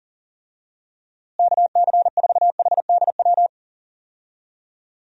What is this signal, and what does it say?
Morse code, heard as a keyed tone, about 37 wpm, 700 Hz. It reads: KC4HDW
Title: USA Call Signs - 6 Characters - Repeated 2x - 40wpm